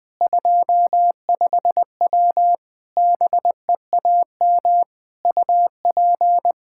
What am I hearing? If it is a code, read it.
Morse code, heard as a keyed tone, about 20 wpm, 700 Hz. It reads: 25W BEAM UP
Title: K1USN SST QSO Practice 20wpm